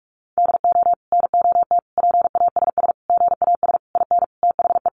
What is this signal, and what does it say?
Morse code, heard as a keyed tone, about 44 wpm, 700 Hz. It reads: DO NOT PASS GAS IN THE
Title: Sentences from Top 1100 Words 45wpm